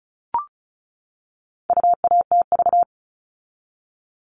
Morse code, heard as a keyed tone, about 34 wpm, 700 Hz. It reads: UATV
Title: All Cut Numbers - 4-Digit 35wpm